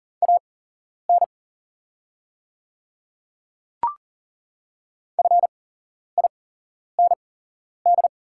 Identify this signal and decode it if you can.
Morse code, heard as a keyed tone, about 41 wpm, spaced as for 11 wpm, 700 Hz. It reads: AN FIND
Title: 8x Character Spacing - Top 100 Words - 40wpm